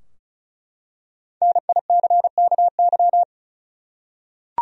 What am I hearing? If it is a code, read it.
Morse code, heard as a keyed tone, about 36 wpm, 700 Hz. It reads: NICKY
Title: USA Common Names - Max 5 Letters - 35wpm